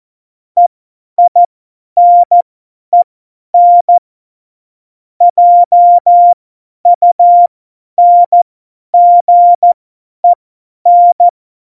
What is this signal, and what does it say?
Morse code, heard as a keyed tone, about 14 wpm, spaced as for 10 wpm, 700 Hz. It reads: EINEN JUNGEN
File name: Morsekurs / Witze / Witz 20. Oktober 2020, 50 F70